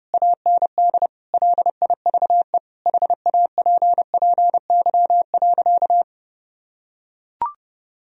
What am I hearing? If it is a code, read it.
Morse code, heard as a keyed tone, about 30 wpm, 700 Hz. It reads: AND LIVE HAPPY.
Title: Sentences from Top 1000 Words - Review Part 3 - 30wpm